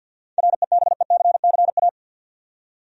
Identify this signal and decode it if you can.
Morse code, heard as a keyed tone, about 50 wpm, 700 Hz. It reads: REBECCA
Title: USA Common Names - 50wpm